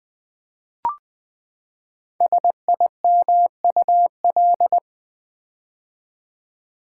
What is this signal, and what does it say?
Morse code, heard as a keyed tone, about 20 wpm, 700 Hz. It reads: SIMUL
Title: Build A Word - 5 Letters At A Time - 10-Letter Words - 20wpm